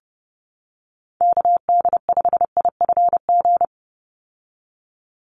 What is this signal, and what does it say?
Morse code, heard as a keyed tone, about 30 wpm, 700 Hz. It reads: KD5IFG
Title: USA Call Signs 30wpm